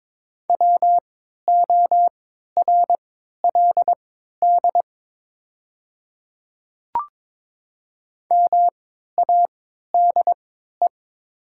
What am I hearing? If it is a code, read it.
Morse code, heard as a keyed tone, about 22 wpm, spaced as for 13 wpm, 700 Hz. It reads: WORLD MADE
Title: 3x Character Spacing - Top 200 Words - 22wpm